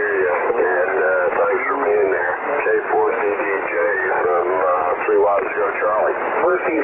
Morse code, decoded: none